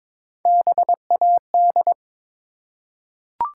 Morse code, heard as a keyed tone, about 22 wpm, 700 Hz. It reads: BAD
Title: Three Letter Words - 22wpm